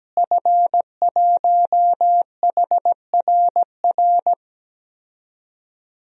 Morse code, heard as a keyed tone, about 17 wpm, 700 Hz. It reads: F1HRR
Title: DXCC-100 Call Signs - 5 Characters - Repeated 3x - 17wpm